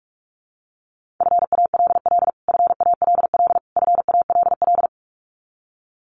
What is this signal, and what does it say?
Morse code, heard as a keyed tone, about 44 wpm, 700 Hz. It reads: FALL FALL FALL
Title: Top 500 Words Repeated 45wpm